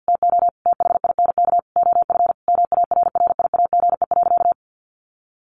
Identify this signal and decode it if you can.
Morse code, heard as a keyed tone, about 50 wpm, 700 Hz. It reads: TO THINK OF MARRIAGE.
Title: Sentences from Top 800 Words 50wpm